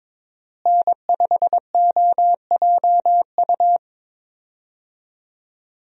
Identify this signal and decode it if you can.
Morse code, heard as a keyed tone, about 22 wpm, 700 Hz. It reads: N5OJU
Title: USA Call Signs - 5 Characters - Repeated 2x - 22wpm